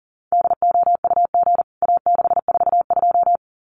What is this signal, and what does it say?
Morse code, heard as a keyed tone, about 40 wpm, 700 Hz. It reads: DOUG A642
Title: CWT Exchange 40wpm